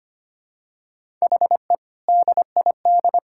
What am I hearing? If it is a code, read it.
Morse code, heard as a keyed tone, about 25 wpm, 700 Hz. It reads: HE DID
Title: Sentences from Top 1000 Words - Review Part 6 - 25wpm